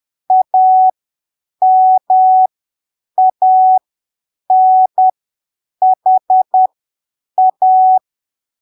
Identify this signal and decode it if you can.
Morse code, heard as a keyed tone, about 10 wpm, spaced as for 7 wpm, 750 Hz. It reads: AMANHA
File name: arquivos / texto / texto 19 10ppm